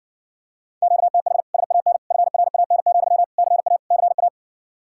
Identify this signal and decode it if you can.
Morse code, heard as a keyed tone, about 60 wpm, 700 Hz. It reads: QTH SAN FRAN, CA CA